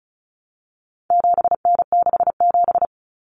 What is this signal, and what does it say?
Morse code, heard as a keyed tone, about 34 wpm, 700 Hz. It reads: 7N67
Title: Common Cut Numbers - 4-Digit 35wpm